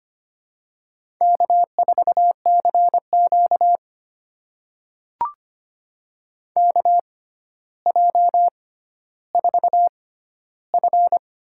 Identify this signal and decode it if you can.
Morse code, heard as a keyed tone, about 25 wpm, spaced as for 13 wpm, 700 Hz. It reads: K4CQ KJ4F
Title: USA Call Signs - 6x Character Spacing - 25wpm